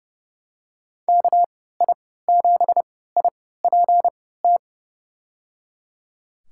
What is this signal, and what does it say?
Morse code, heard as a keyed tone, about 31 wpm, spaced as for 17 wpm, 700 Hz. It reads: KI7IPT
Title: USA Call Signs - 3x Character Spacing - 30wpm